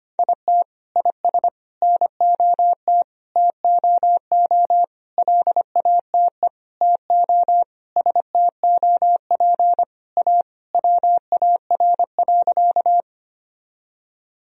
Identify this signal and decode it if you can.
Morse code, heard as a keyed tone, about 25 wpm, 700 Hz. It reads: IT IS NOT TOO LATE TO STOP A WAR.
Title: Sentences from Top 1000 Words - Review Part 2 - 25wpm